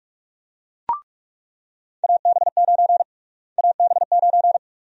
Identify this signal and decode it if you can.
Morse code, heard as a keyed tone, about 46 wpm, 700 Hz. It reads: AB9 AB9
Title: USA Call Sign Prefixes - Repeated 2x - 45wpm